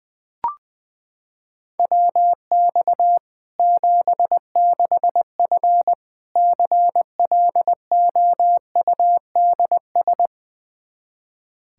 Morse code, heard as a keyed tone, about 21 wpm, 700 Hz. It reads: WX 76F CLOUDS
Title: Practice QSO - Exchange 2 - Info - 20wpm